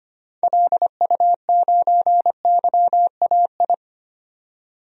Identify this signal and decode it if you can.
Morse code, heard as a keyed tone, about 25 wpm, 700 Hz. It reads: LU9YAI
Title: DXCC-100 Call Signs - 6 Characters - 25wpm